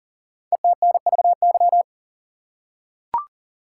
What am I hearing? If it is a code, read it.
Morse code, heard as a keyed tone, about 43 wpm, 700 Hz. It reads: ETNVY
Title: Code Groups - 5 Letters - 40wpm